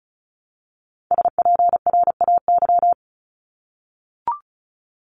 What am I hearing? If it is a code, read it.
Morse code, heard as a keyed tone, about 34 wpm, 700 Hz. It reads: SPRAY
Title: Top 1751-2000 Words 35wpm